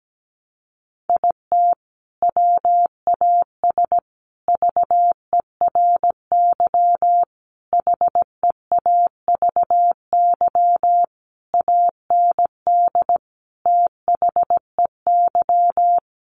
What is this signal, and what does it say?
Morse code, heard as a keyed tone, about 17 wpm, 700 Hz. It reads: IT WAS VERY HEAVY AND THEY
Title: Sentences from Top 1100 Words 17wpm